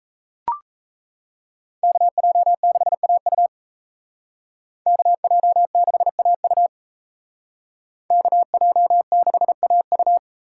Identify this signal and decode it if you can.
Morse code, heard as a keyed tone, about 36 wpm, 700 Hz. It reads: KJ6AU KJ6AU KJ6AU